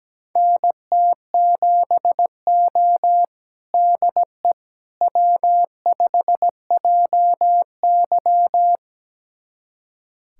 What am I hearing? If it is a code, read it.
Morse code, heard as a keyed tone, about 17 wpm, 700 Hz. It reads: NT7O DE W5JY
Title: USA Callsign DE Callsign - 4 Characters - 17wpm